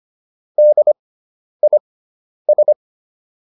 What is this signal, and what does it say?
Morse code, heard as a keyed tone, about 25 wpm, spaced as for 10 wpm, 600 Hz. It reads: DIS